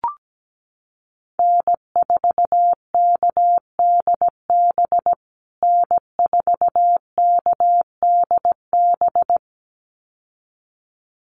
Morse code, heard as a keyed tone, about 17 wpm, 700 Hz. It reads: N4KDB N4KDB
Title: USA Call Signs Repeated 2x 17wpm